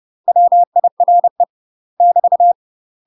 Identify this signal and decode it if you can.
Morse code, heard as a keyed tone, about 30 wpm, 700 Hz. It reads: WIRE <BT>
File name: UK MASTER QSOs TRAINING / 30WPM / UK MASTER QSOs 0011